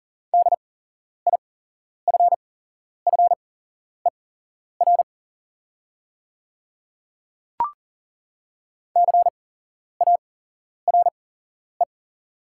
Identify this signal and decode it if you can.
Morse code, heard as a keyed tone, about 39 wpm, spaced as for 11 wpm, 690 Hz. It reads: DIFFER CARE
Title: 8x Character Spacing - Top 300 Words - 40wpm